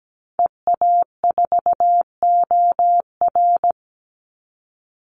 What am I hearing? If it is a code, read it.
Morse code, heard as a keyed tone, about 17 wpm, 700 Hz. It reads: EA4OR
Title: Call Signs - 17wpm